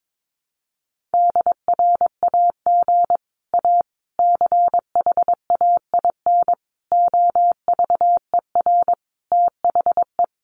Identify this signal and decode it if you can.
Morse code, heard as a keyed tone, about 22 wpm, 700 Hz. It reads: DRAG A CHAIN OVER THE
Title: Sentences from Top 1500 Words 22wpm